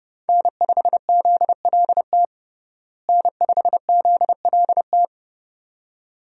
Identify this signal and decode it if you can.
Morse code, heard as a keyed tone, about 32 wpm, 700 Hz. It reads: N5ZLT N5ZLT
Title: USA Call Signs - 5 Characters - Repeated 2x - 30wpm